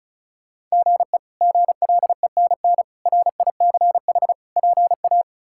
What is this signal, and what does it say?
Morse code, heard as a keyed tone, about 34 wpm, 700 Hz. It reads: GE GLENN RICH PA
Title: K1USN SST QSO Practice - 35wpm